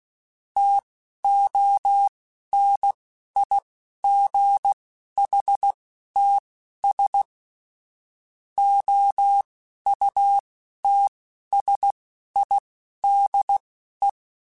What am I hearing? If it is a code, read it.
Morse code, heard as a keyed tone, about 16 wpm, spaced as for 11 wpm, 800 Hz. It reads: TONIGHTS OUTSIDE
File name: Spaced Characters / Spaced Characters-mp3s / Weather4 21 sp @15